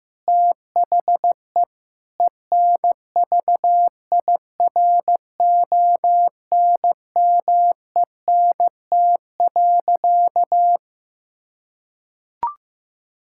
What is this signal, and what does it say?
Morse code, heard as a keyed tone, about 15 wpm, 700 Hz. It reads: THE ENVIRONMENT.
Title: Sentences from Top 1000 Words - Review Part 6 - 15wpm